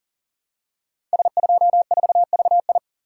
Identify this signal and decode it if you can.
Morse code, heard as a keyed tone, about 41 wpm, 700 Hz. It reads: S24VI
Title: Code Groups - 5 Letter:Numbers - 40wpm